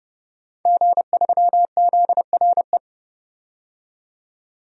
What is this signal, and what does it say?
Morse code, heard as a keyed tone, about 32 wpm, 700 Hz. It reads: G3ZRE
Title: DXCC-100 Call Signs - 5 Characters - 30wpm